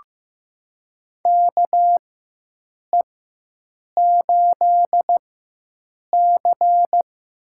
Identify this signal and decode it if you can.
Morse code, heard as a keyed tone, about 15 wpm, spaced as for 7 wpm, 700 Hz. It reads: KE8C
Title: USA Call Signs - 4x Character Spacing - 15wpm